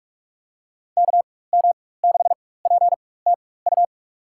Morse code, heard as a keyed tone, about 46 wpm, spaced as for 21 wpm, 700 Hz. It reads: KM6PTU
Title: USA Call Signs - 6 Characters - 4x Character Spacing - 45wpm